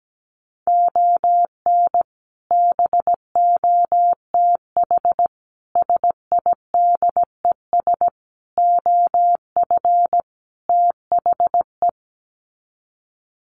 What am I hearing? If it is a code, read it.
Morse code, heard as a keyed tone, about 17 wpm, 700 Hz. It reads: ON BOTH SIDES OF THE
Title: Sets of 5 Words 17wpm